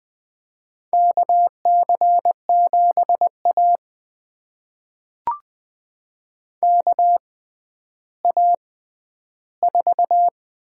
Morse code, heard as a keyed tone, about 20 wpm, spaced as for 11 wpm, 700 Hz. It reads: KC7A KA4
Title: USA Call Signs - 4 Characters - 6x Character Spacing - 20wpm